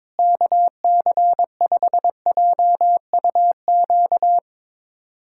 Morse code, heard as a keyed tone, about 22 wpm, 700 Hz. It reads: KC5JUQ
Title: USA Callsign DE Callsign - 6 Characters - 22wpm